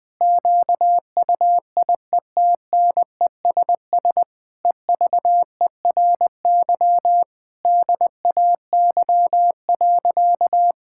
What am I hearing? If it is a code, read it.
Morse code, heard as a keyed tone, about 20 wpm, 700 Hz. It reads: QUIETNESS EVERY DAY.
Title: Difficult Sentences with Hints - 20wpm